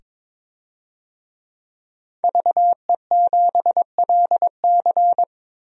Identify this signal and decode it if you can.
Morse code, heard as a keyed tone, about 21 wpm, 700 Hz. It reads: VE7LC
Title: DXCC-100 Call Signs - 5 Characters - 22wpm